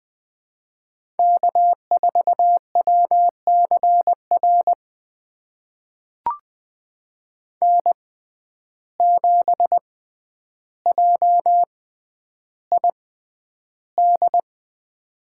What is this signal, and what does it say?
Morse code, heard as a keyed tone, about 21 wpm, spaced as for 10 wpm, 700 Hz. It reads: K4WCR N7JID
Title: USA Call Signs - 6x Character Spacing - 20wpm